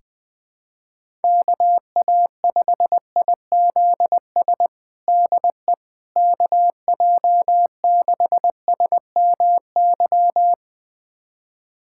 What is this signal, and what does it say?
Morse code, heard as a keyed tone, about 21 wpm, 700 Hz. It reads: KA5IZS DE KJ6SMY
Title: USA Callsign DE Callsign - 6 Characters - 20wpm